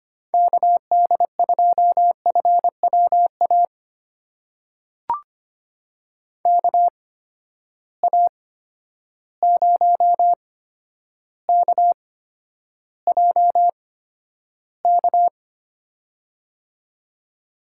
Morse code, heard as a keyed tone, about 25 wpm, spaced as for 11 wpm, 700 Hz. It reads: KD2FWA KA0KJK
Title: USA Call Signs - 6 Characters - 8x Character Spacing - 25wpm